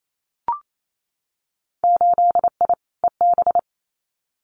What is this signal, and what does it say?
Morse code, heard as a keyed tone, about 28 wpm, 700 Hz. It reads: 8I EB
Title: Winter Field Day Exchanges Encore 28wpm